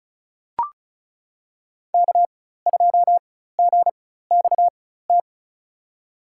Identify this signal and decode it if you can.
Morse code, heard as a keyed tone, about 34 wpm, spaced as for 16 wpm, 700 Hz. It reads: K2GXT
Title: USA Call Signs - 5 Characters - 4x Character Spacing - 35wpm